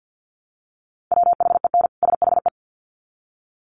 Morse code, heard as a keyed tone, about 50 wpm, 700 Hz. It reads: WHEN SHE
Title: Sets of 2 Words 50wpm